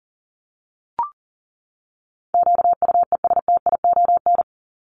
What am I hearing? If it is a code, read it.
Morse code, heard as a keyed tone, about 40 wpm, 700 Hz. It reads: QUESTION
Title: ES:S Suffix - Top 100 Words with Base Words 40wpm